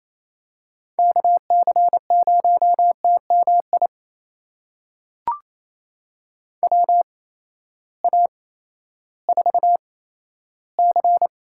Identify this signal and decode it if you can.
Morse code, heard as a keyed tone, about 28 wpm, spaced as for 14 wpm, 700 Hz. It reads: KC0TMI WA4C
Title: USA Call Signs - 6 Characters - 8x Character Spacing - 28wpm